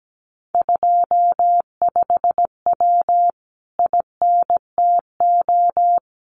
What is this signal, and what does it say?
Morse code, heard as a keyed tone, about 17 wpm, 700 Hz. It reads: 25W INTO
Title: Common QSO Phrases Encore 17wpm